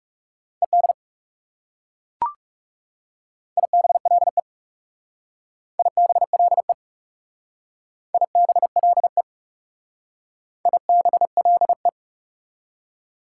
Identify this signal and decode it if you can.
Morse code, heard as a keyed tone, about 38 wpm, 700 Hz. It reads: ED IBLE IBLE IBLE IBLE